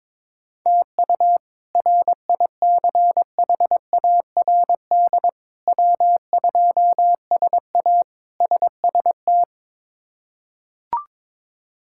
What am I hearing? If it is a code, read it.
Morse code, heard as a keyed tone, about 22 wpm, 700 Hz. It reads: TU RICHARD W2SA SST
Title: K1USN SST QSO Practice - 22wpm